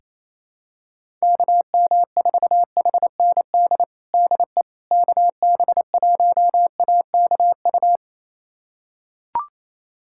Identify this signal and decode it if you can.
Morse code, heard as a keyed tone, about 28 wpm, 700 Hz. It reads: KM4HND DE KB1AKU
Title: USA Callsign DE Callsign 28wpm